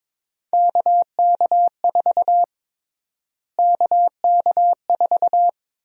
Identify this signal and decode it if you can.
Morse code, heard as a keyed tone, about 22 wpm, 700 Hz. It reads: KK4 KK4
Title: USA Call Sign Prefixes - Repeated 2x - 22wpm